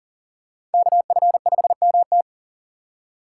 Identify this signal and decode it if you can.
Morse code, heard as a keyed tone, about 40 wpm, 700 Hz. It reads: KF5MT
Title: USA Call Signs - 5 Characters - 4x Character Spacing - 40wpm